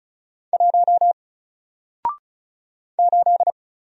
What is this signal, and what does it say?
Morse code, heard as a keyed tone, about 36 wpm, 700 Hz. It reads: 1 8
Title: Single Character (Letter:Number:Punctuation:Prosign) 35wpm